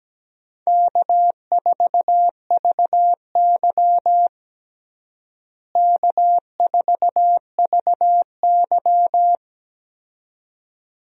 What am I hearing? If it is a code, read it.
Morse code, heard as a keyed tone, about 17 wpm, 700 Hz. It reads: K4VY K4VY
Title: USA Call Signs - 4 Characters - Repeated 2x - 17wpm